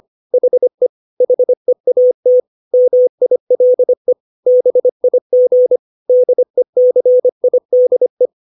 Morse code, heard as a keyed tone, about 25 wpm, 500 Hz. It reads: HE HEAT MILE BIG DECIDE
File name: 632 words / 632 Words 25WPM